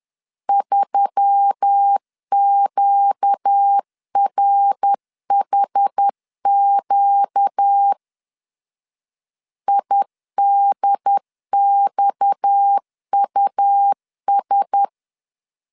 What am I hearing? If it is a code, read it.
Morse code, heard as a keyed tone, about 10 wpm, 800 Hz. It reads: 3QRHQ IDXUS
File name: CW / DET / 1 6 1500C S B 10wpm 4137